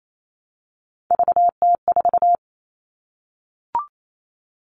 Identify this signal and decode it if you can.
Morse code, heard as a keyed tone, about 28 wpm, 700 Hz. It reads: VT4
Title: All Cut Numbers - 3-Digit 28wpm